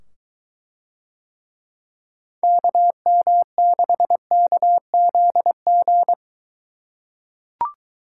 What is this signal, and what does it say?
Morse code, heard as a keyed tone, about 22 wpm, 700 Hz. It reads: KM6KZG